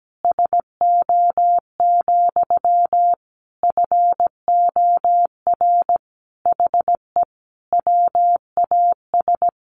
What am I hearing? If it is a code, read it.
Morse code, heard as a keyed tone, about 17 wpm, 700 Hz. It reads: SO, FOR HE WAS
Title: Sentences from Top 1500 Words 17wpm